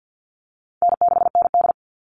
Morse code, heard as a keyed tone, about 50 wpm, 700 Hz. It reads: N6ND
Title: All Cut Numbers - 4-Digit 50wpm